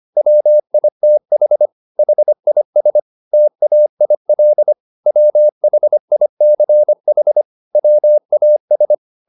Morse code, heard as a keyed tone, about 25 wpm, 600 Hz. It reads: WITH HIS TAIL WHICH WAS
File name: the old man and the sea part 1 25wpm